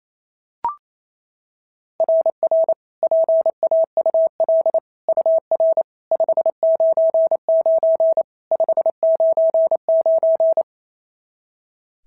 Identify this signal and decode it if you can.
Morse code, heard as a keyed tone, about 29 wpm, 660 Hz. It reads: RR PAUL UR 599 599
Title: SOTA QSO Practice 28wpm